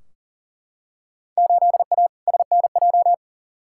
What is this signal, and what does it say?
Morse code, heard as a keyed tone, about 40 wpm, 700 Hz. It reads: 8A SNJ
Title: Field Day Exchanges Encore - 40wpm